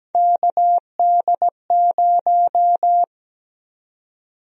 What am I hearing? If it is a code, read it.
Morse code, heard as a keyed tone, about 17 wpm, 700 Hz. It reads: KD0
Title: USA Call Sign Prefixes - 17wpm